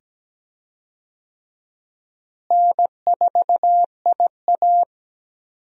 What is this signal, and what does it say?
Morse code, heard as a keyed tone, about 17 wpm, 700 Hz. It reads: N4IA
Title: USA Call Signs - 4 Characters - Repeated 2x - 17wpm